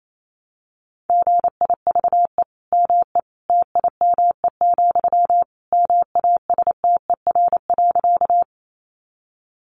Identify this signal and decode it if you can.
Morse code, heard as a keyed tone, about 28 wpm, 700 Hz. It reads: GIVE ME TIME, MASTER.
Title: Sentences from Top 1300 Words 28wpm